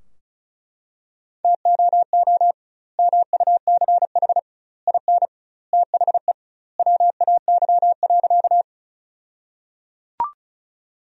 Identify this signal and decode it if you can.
Morse code, heard as a keyed tone, about 36 wpm, 700 Hz. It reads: TOO MUCH IN THE WAY.
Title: 2x Word-Spacing - Sentences from Top 200 Words - 35wpm